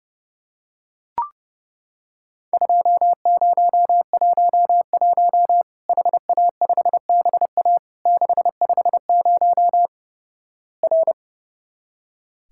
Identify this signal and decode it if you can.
Morse code, heard as a keyed tone, about 30 wpm, 700 Hz. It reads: 2011 HA5BA 650 R
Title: Worked All Europe DX Contest - QTC Exchange - 30wpm